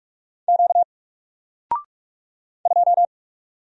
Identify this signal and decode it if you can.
Morse code, heard as a keyed tone, about 42 wpm, 700 Hz. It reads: Q 2
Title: Single Letter-Number 45wpm